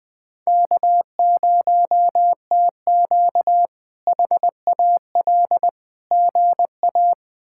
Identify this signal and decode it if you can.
Morse code, heard as a keyed tone, about 20 wpm, 700 Hz. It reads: K0TQ HAL GA
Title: K1USN SST QSO Practice - 20wpm